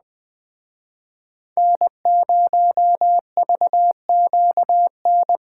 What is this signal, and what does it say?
Morse code, heard as a keyed tone, about 20 wpm, 700 Hz. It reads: N0VQN